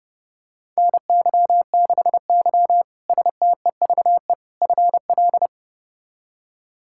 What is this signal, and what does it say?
Morse code, heard as a keyed tone, about 30 wpm, 700 Hz. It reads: NY6Y STEVE FL
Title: K1USN SST QSO Practice - 30wpm